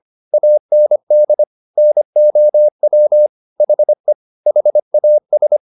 Morse code, heard as a keyed tone, about 25 wpm, 600 Hz. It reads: AND NOW HE HAS